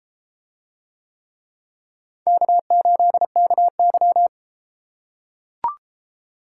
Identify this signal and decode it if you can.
Morse code, heard as a keyed tone, about 30 wpm, 700 Hz. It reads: K8KY